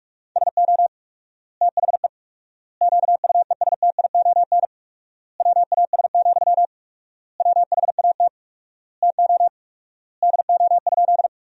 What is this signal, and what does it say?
Morse code, heard as a keyed tone, about 41 wpm, 700 Hz. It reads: SO THE QUESTION WAS, WHAT TO DO?
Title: 4x Word-Spacing - Sentences from Top 400 Words - 45wpm